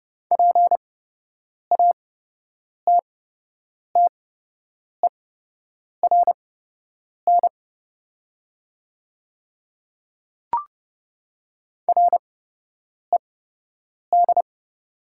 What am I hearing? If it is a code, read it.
Morse code, heard as a keyed tone, about 30 wpm, spaced as for 8 wpm, 700 Hz. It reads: PATTERN RED
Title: 8x Character Spacing - Top 500 Words - 30wpm